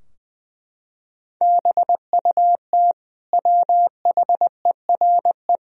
Morse code, heard as a keyed tone, about 20 wpm, 700 Hz. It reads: BUT WHERE
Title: Sentences from Top 1000 Words - Review Part 2 - 20wpm